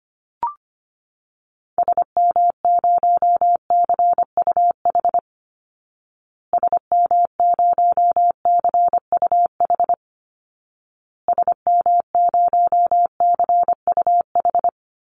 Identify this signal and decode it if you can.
Morse code, heard as a keyed tone, about 25 wpm, 700 Hz. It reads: SM0CUH SM0CUH SM0CUH
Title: Call Signs Repeated 2x - 25wpm